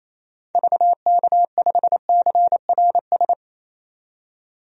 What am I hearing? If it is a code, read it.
Morse code, heard as a keyed tone, about 27 wpm, 700 Hz. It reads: VK5CRS
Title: DXCC-100 Call Signs - 6 Characters - 28wpm